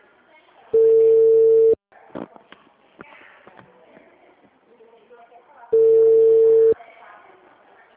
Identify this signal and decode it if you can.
Ringback tone (China, 450 Hz), with speech.
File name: out- -2015- - - .34